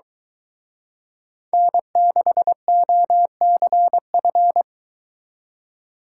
Morse code, heard as a keyed tone, about 23 wpm, 700 Hz. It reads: N6OCF